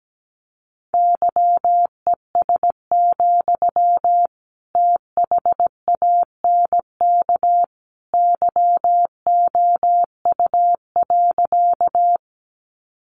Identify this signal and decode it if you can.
Morse code, heard as a keyed tone, about 17 wpm, 700 Hz. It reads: YES, THANK YOU.
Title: Sentences from Top 600 Words 17wpm